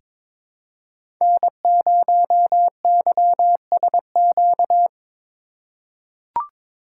Morse code, heard as a keyed tone, about 22 wpm, 700 Hz. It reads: N0YSQ
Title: K1USN SST QSO Practice - 22wpm